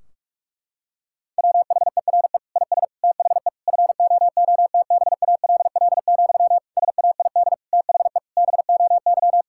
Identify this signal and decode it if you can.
Morse code, heard as a keyed tone, about 42 wpm, 700 Hz. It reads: WHERE IS THE FOOTBALL, SAID THE BOY
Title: Sentences from Top 1600 Words 45wpm